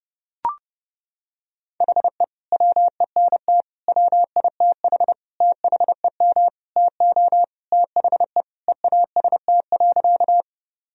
Morse code, heard as a keyed tone, about 30 wpm, 700 Hz. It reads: HE WENT WITH THEM TO THE EAST.
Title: Sentences from Top 1000 Words - Review Part 3 - 30wpm